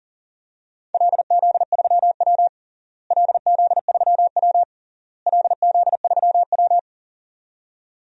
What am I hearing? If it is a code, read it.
Morse code, heard as a keyed tone, about 40 wpm, 700 Hz. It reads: LZ3W LZ3W LZ3W
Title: DXCC-100 Call Signs - 4 Characters - Repeated 3x - 40wpm